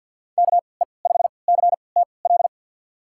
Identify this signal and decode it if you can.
Morse code, heard as a keyed tone, about 49 wpm, spaced as for 28 wpm, 700 Hz. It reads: KE5CTL
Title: USA Call Signs - 6 Characters - 3x Character Spacing - 50wpm